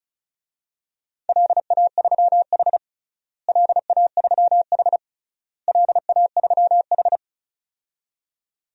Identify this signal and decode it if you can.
Morse code, heard as a keyed tone, about 36 wpm, 700 Hz. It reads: LA3H LA3H LA3H
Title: DXCC-100 Call Signs - 4 Characters - Repeated 3x - 35wpm